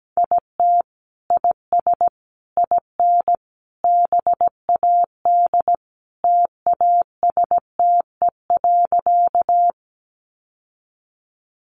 Morse code, heard as a keyed tone, about 17 wpm, 700 Hz. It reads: IT IS IN BAD TASTE.
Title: Sentences from Top 1100 Words 17wpm